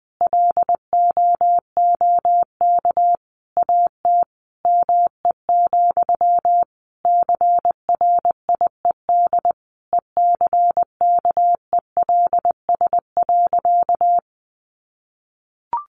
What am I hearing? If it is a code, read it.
Morse code, heard as a keyed tone, about 20 wpm, 700 Hz. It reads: LOOK AT ME, CRIED ECKELS.
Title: A Sound Of Thunder 20wpm